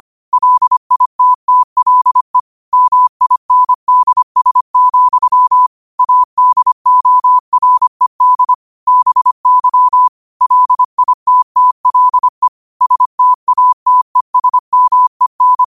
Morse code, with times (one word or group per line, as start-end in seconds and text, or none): 0.3-2.4 LITTLE
2.7-5.7 MINDS,
6.0-8.5 ADORED
8.9-10.1 BY
10.4-12.5 LITTLE
12.8-15.6 STATESMEN